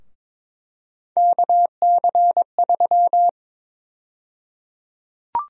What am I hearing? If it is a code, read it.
Morse code, heard as a keyed tone, about 22 wpm, 700 Hz. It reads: KC3
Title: USA Call Sign Prefixes - Repeated 2x - 22wpm